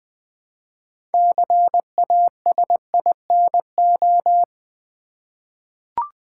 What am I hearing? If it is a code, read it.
Morse code, heard as a keyed tone, about 20 wpm, 700 Hz. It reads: CASINO
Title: 6 Letter Words - First 3 Letters Given - 20wpm